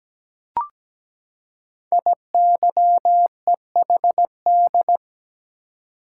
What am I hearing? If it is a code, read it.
Morse code, heard as a keyed tone, about 17 wpm, 700 Hz. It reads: IYEHD
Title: Code Groups - 5 Letters - 17wpm